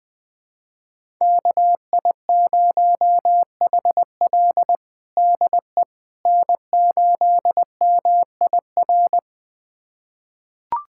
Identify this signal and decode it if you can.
Morse code, heard as a keyed tone, about 20 wpm, 700 Hz. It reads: KI0HL DE N8MIR
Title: USA Callsign DE Callsign - 5 Characters - 20wpm